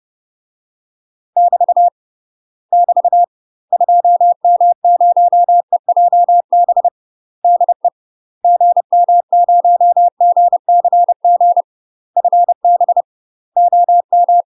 Morse code, heard as a keyed tone, about 30 wpm, 700 Hz. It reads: <BT> <BT> 2M0EJB DE GM0GCG FB OM
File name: UK MASTER QSOs TRAINING/30WPM